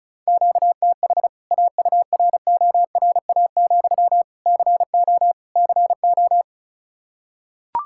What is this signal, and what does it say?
Morse code, heard as a keyed tone, about 36 wpm, 700 Hz. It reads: QTH AURORA, CO CO
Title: K1USN SST QSO Practice 35wpm